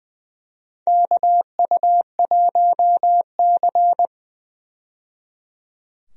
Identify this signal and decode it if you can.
Morse code, heard as a keyed tone, about 20 wpm, 700 Hz. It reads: KU1C
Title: USA Call Signs - 4 Characters - 20wpm